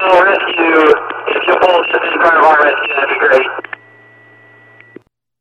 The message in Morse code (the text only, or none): none